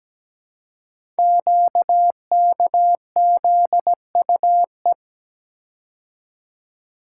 Morse code, heard as a keyed tone, about 17 wpm, 700 Hz. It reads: QKZUE
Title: Code Groups - 5 Letters - 17wpm